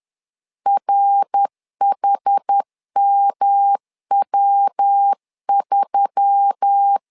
Morse code, heard as a keyed tone, about 10 wpm, 800 Hz. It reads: RHMW3